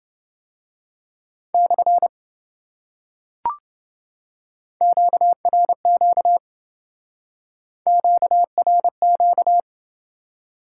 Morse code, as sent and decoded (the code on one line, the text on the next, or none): -..-. / --.- .-. --.- / --.- .-. --.-
/ QRQ QRQ